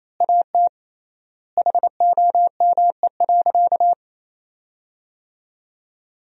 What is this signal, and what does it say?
Morse code, heard as a keyed tone, about 28 wpm, 700 Hz. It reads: AT HOME.
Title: 3x Word-Spacing - Sentences from Top 400 Words - 28wpm